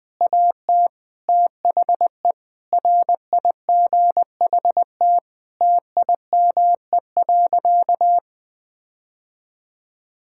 Morse code, heard as a keyed tone, about 20 wpm, 700 Hz. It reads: AT THE RIGHT TIME.
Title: Sentences with 6-Word Phrases - 20wpm